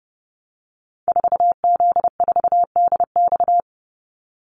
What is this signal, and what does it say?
Morse code, heard as a keyed tone, about 30 wpm, 700 Hz. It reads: 4Z4DX
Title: Call Signs Repeated 3x - 30wpm